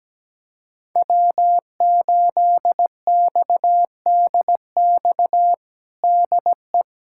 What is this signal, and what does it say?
Morse code, heard as a keyed tone, about 17 wpm, 700 Hz. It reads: W8XDX DE
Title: USA Callsign DE Callsign - 5 Characters - 17wpm